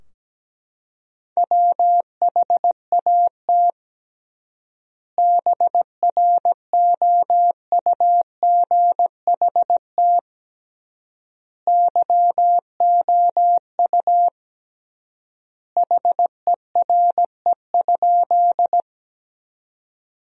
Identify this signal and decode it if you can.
Morse code, heard as a keyed tone, about 17 wpm, 700 Hz. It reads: WHAT BROUGHT YOU HERE?
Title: 3x Word-Spacing - Sentences from Top 500 Words - 17wpm